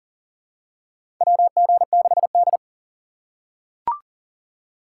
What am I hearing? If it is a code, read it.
Morse code, heard as a keyed tone, about 40 wpm, 700 Hz. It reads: WG6D
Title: USA Call Signs - 6x Character Spacing - 40wpm